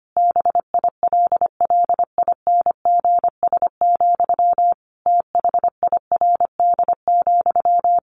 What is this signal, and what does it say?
Morse code, heard as a keyed tone, about 25 wpm, 700 Hz. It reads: BILLINGS, THIRD,
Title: A Sound Of Thunder 25wpm